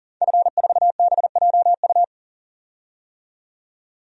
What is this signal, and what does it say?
Morse code, heard as a keyed tone, about 40 wpm, 700 Hz. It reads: F4BJU
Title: DXCC-100 Call Signs - 5 Characters - 40wpm